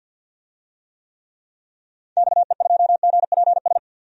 Morse code, heard as a keyed tone, about 51 wpm, 700 Hz. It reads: XE2GPS
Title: DXCC-100 Call Signs - 6 Characters - 50wpm